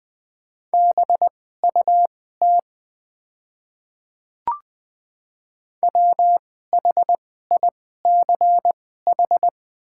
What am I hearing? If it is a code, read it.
Morse code, heard as a keyed tone, about 20 wpm, spaced as for 14 wpm, 700 Hz. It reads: BUT WHICH
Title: 2x Character Spacing - Top 300 Words - 20wpm